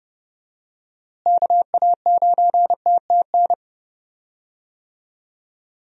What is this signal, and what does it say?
Morse code, heard as a keyed tone, about 31 wpm, 700 Hz. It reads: KA9TTN
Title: USA Call Signs - 6 Characters - Repeated 2x - 30wpm